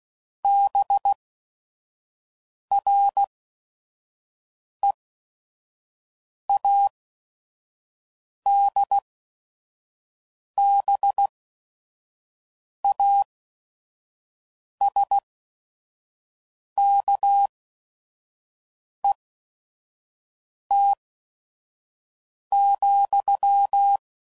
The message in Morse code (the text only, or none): BREADBASKET,